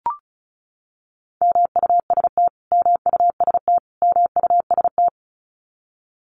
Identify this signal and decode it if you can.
Morse code, heard as a keyed tone, about 34 wpm, 700 Hz. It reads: MUST MUST MUST
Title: Top 300 Words Repeated 35wpm